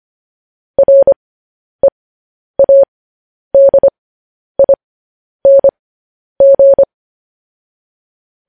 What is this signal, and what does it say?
Morse code, heard as a keyed tone, about 25 wpm, spaced as for 10 wpm, 550 Hz. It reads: READING
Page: news-headlines-in-morse-code-at-10-wpm